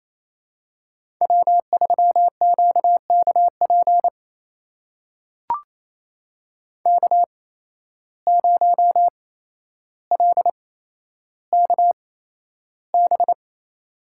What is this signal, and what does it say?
Morse code, heard as a keyed tone, about 28 wpm, spaced as for 12 wpm, 700 Hz. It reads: W3QKP K0LKB
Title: USA Call Signs - 5 Characters - 8x Character Spacing - 28wpm